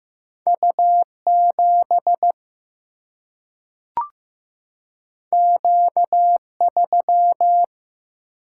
Morse code, heard as a keyed tone, about 15 wpm, 700 Hz. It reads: U 7 Q 3
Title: Single-Letter Single-Number 15wpm